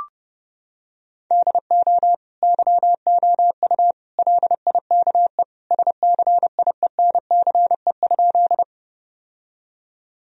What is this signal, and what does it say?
Morse code, heard as a keyed tone, about 30 wpm, 700 Hz. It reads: DO YOU LIKE SCIENCE?
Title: Sentences from Top 1000 Words - Review Part 2 - 30wpm